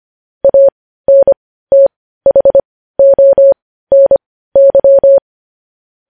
Morse code, heard as a keyed tone, about 25 wpm, spaced as for 15 wpm, 550 Hz. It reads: ANTHONY
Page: news-headlines-in-morse-code-at-15-wpm